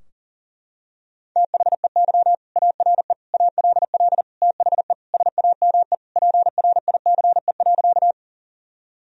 Morse code, heard as a keyed tone, about 37 wpm, 700 Hz. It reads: THEY ARE ALL THE SAME PRICE.
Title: Sentences from Top 1000 Words - Review Part 4 - 40wpm